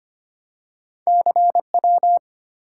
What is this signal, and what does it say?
Morse code, heard as a keyed tone, about 25 wpm, 700 Hz. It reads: CW
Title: Top 180 QSO Elements 25wpm